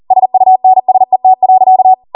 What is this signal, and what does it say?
Morse code, heard as a keyed tone, about 40 wpm, 750 Hz. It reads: SUNSET.